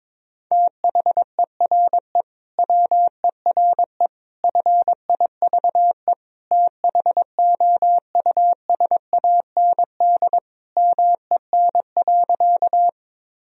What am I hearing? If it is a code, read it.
Morse code, heard as a keyed tone, about 22 wpm, 700 Hz. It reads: THERE WERE FIVE THOUSAND MEN.
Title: Sentences from Top 1000 Words - Review Part 3 - 22wpm